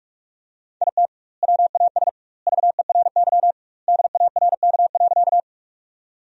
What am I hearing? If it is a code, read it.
Morse code, heard as a keyed tone, about 46 wpm, 700 Hz. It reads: IT WAS VERY DARK.
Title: 2x Word-Spacing - Sentences from Top 500 Words - 45wpm